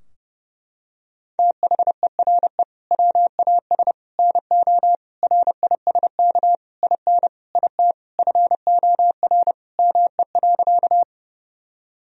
Morse code, heard as a keyed tone, about 32 wpm, 700 Hz. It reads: THERE WAS NO RISK IN IT FOR ME.
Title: Sentences from Top 1000 Words - Review Part 4 - 30wpm